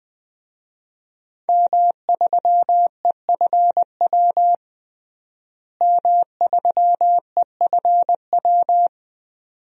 Morse code, heard as a keyed tone, about 20 wpm, 700 Hz. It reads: M3EFW M3EFW
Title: DXCC-100 Call Signs - 5 Characters - Repeated 2x - 20wpm